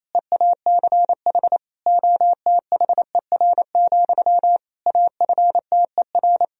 Morse code, heard as a keyed tone, about 28 wpm, 700 Hz. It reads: EACH OTHER, AFTER
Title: Sentences from Top 1000 Words - Review Part 1 - 28wpm